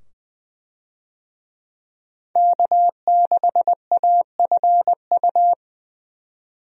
Morse code, heard as a keyed tone, about 21 wpm, 700 Hz. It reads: K6AFU
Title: USA Call Signs - 5 Characters - 20wpm